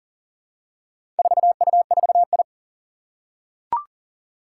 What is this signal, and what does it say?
Morse code, heard as a keyed tone, about 40 wpm, 700 Hz. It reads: 4U4I
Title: Four Challenging Letter-Number Combinations 40wpm